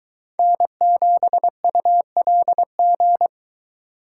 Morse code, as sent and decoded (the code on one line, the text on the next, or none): -. --... ..- .-.. --.
N7ULG